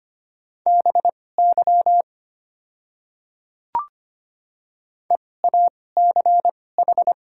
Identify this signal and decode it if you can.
Morse code, heard as a keyed tone, about 25 wpm, spaced as for 18 wpm, 700 Hz. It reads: BY EACH
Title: 2x Character Spacing - Top 100 Words - 25wpm